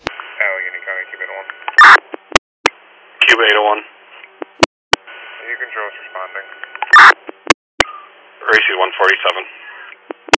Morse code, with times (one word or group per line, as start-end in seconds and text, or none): none